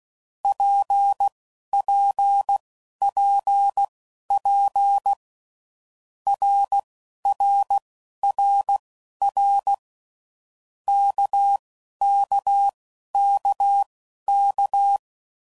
Morse code, heard as a keyed tone, about 16 wpm, spaced as for 11 wpm, 800 Hz. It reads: PPPP RRRR KKKK
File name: Groups / Basic Letter Groups / RKPX